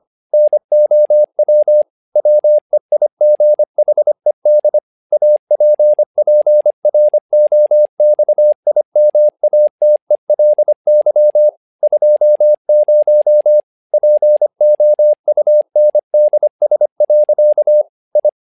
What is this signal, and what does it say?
Morse code, heard as a keyed tone, about 25 wpm, 600 Hz. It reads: NOW WEIGHED APPROXIMATELY 20 POUNDS. I
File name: learning center / tower trouble / tower trouble 25wpm